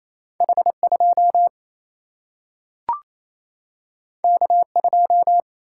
Morse code, heard as a keyed tone, about 28 wpm, 700 Hz. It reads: H2 K2
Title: Single-Letter Single-Number 28wpm